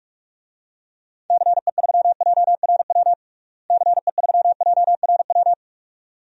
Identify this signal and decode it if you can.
Morse code, heard as a keyed tone, about 46 wpm, 700 Hz. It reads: CE3JRW CE3JRW
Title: DXCC-100 Call Signs - 6 Characters - Repeated 2x - 45wpm